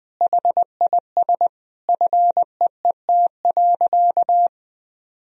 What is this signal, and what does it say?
Morse code, heard as a keyed tone, about 21 wpm, 700 Hz. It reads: HIS FEET.
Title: Sentences from Top 1000 Words - Review Part 1 - 20wpm